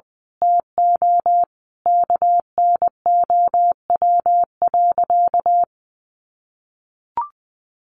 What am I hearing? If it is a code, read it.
Morse code, heard as a keyed tone, about 20 wpm, 700 Hz. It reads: TO KNOW.
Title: Sentences from Top 1200 Words 20wpm